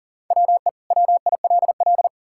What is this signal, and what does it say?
Morse code, heard as a keyed tone, about 40 wpm, 700 Hz. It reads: WE WILL
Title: Sentences from Top 1000 Words - Review Part 5 - 40wpm